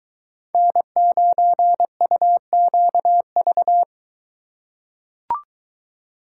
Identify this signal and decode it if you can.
Morse code, heard as a keyed tone, about 22 wpm, 700 Hz. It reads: N9UQV